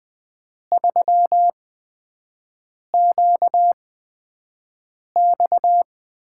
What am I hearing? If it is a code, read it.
Morse code, heard as a keyed tone, about 20 wpm, spaced as for 5 wpm, 700 Hz. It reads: 3QX
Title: USA Call Signs - 8x Character Spacing - 20wpm